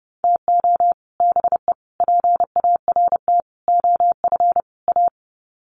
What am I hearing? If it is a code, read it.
Morse code, heard as a keyed tone, about 30 wpm, 700 Hz. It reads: TO BE PART OF A
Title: Sets of 5 Words Encore2 30wpm